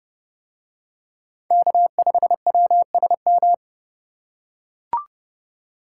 Morse code, heard as a keyed tone, about 29 wpm, 700 Hz. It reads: K5WSM